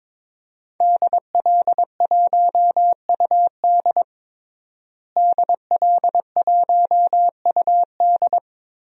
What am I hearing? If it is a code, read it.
Morse code, heard as a keyed tone, about 22 wpm, 700 Hz. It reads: DL1UD DL1UD
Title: DXCC-100 Call Signs - 5 Characters - Repeated 3x - 22wpm